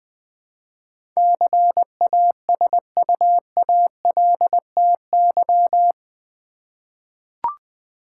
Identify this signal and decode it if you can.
Morse code, heard as a keyed tone, about 20 wpm, 700 Hz. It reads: CASUALTY
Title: Eight Letter Words - 20wpm